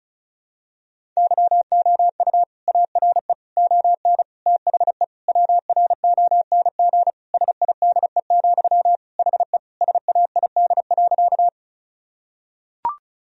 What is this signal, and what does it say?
Morse code, heard as a keyed tone, about 36 wpm, 700 Hz. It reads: YOU ARE ON THE WRONG SIDE, HE SAID.
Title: Sentences from Top 1000 Words - Review Part 1 - 35wpm